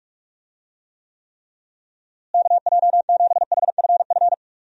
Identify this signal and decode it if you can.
Morse code, heard as a keyed tone, about 46 wpm, 700 Hz. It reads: KJ7HFF
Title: USA Call Signs - 6 Characters - Repeated 3x - 45wpm